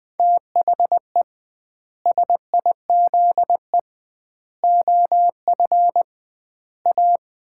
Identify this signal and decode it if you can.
Morse code, heard as a keyed tone, about 21 wpm, 700 Hz. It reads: THE SIZE OF A
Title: 2x Word-Spacing - Sets of 4 Words - 20wpm